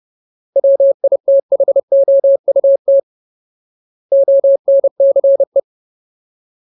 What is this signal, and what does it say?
Morse code, heard as a keyed tone, about 30 wpm, 550 Hz. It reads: WITHOUT ONCE
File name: Ultimate Word List / 30WPM Ultimate